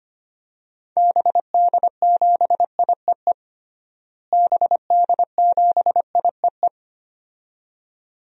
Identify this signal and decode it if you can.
Morse code, heard as a keyed tone, about 25 wpm, 700 Hz. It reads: BD7IEE BD7IEE
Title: DXCC-100 Call Signs - 6 Characters - Repeated 2x - 25wpm